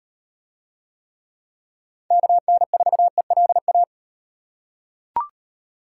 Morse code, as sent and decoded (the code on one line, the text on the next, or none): -.- -. ....- . .-.. .-
KN4ELA